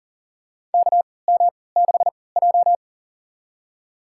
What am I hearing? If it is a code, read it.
Morse code, heard as a keyed tone, about 37 wpm, spaced as for 22 wpm, 700 Hz. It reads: KM6J
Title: USA Call Signs - 4 Characters - 3x Character Spacing - 40wpm